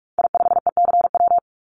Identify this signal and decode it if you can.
Morse code, heard as a keyed tone, about 44 wpm, 700 Hz. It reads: I5ECW
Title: Call Signs - 45wpm